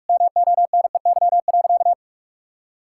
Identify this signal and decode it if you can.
Morse code, heard as a keyed tone, about 46 wpm, 700 Hz. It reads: MONEY.
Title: Sentences from Top 1000 Words - Review Part 5 - 45wpm